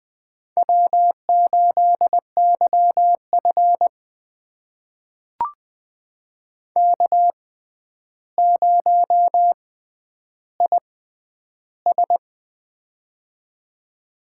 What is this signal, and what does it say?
Morse code, heard as a keyed tone, about 20 wpm, spaced as for 10 wpm, 700 Hz. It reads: W8YF K0IS
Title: USA Call Signs - 4 Characters - 6x Character Spacing - 20wpm